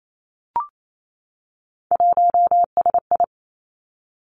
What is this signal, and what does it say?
Morse code, heard as a keyed tone, about 28 wpm, 700 Hz. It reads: 1SI
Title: Three Challenging Letter-Number Combinations 28wpm